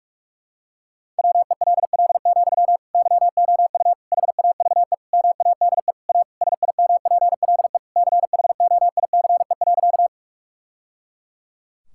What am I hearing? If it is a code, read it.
Morse code, heard as a keyed tone, about 46 wpm, 700 Hz. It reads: WELL, YOU HAVE MADE A SIMPLE CHOICE.
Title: Sentences from Top 1000 Words - Review Part 6 - 45wpm